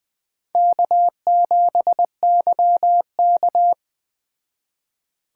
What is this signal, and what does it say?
Morse code, heard as a keyed tone, about 20 wpm, 700 Hz. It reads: K7YK
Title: Speed Racing - USA 1x2 Call Signs 17wpm